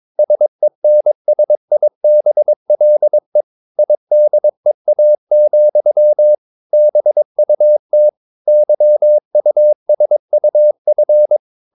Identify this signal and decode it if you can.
Morse code, heard as a keyed tone, about 22 wpm, 600 Hz. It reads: SENSIBLE IDEA, BUT YUSUF
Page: Romanen Never af Ken Follet på engelsk.